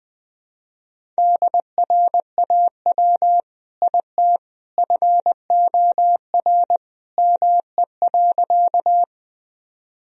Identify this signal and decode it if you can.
Morse code, heard as a keyed tone, about 21 wpm, 700 Hz. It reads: DRAW IT FOR ME.
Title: Sentences from Top 1000 Words - Review Part 1 - 20wpm